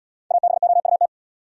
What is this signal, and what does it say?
Morse code, heard as a keyed tone, about 77 wpm, 700 Hz. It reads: ABOUT